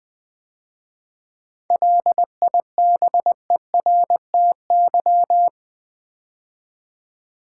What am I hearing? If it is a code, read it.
Morse code, heard as a keyed tone, about 20 wpm, 700 Hz. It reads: LIBERTY
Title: 7 Letter Words - First 3 Letters Given - 20wpm